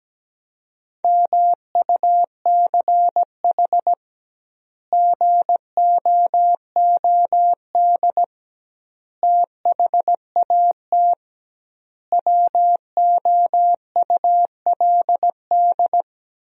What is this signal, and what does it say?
Morse code, heard as a keyed tone, about 17 wpm, 700 Hz. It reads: MUCH GOOD THAT WOULD
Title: 2x Word-Spacing - Sentences from Top 200 Words - 17wpm